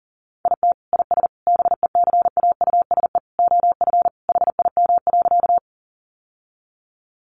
Morse code, heard as a keyed tone, about 40 wpm, 700 Hz. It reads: IT IS BECAUSE OF HIM.
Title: Sentences from Top 600 Words 40wpm